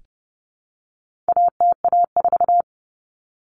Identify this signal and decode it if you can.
Morse code, heard as a keyed tone, about 30 wpm, 700 Hz. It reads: ATA4
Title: Common Cut Numbers - 4-Digit 30wpm